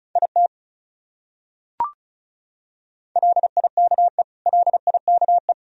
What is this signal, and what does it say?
Morse code, heard as a keyed tone, about 34 wpm, 700 Hz. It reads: IT LIKE LIKE
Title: Top 100 Words Repeated 35wpm